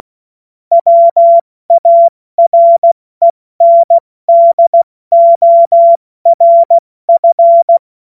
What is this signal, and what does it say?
Morse code, heard as a keyed tone, about 16 wpm, 680 Hz. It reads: WARENDORF
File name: Witz-22-Juli-2025-70-F80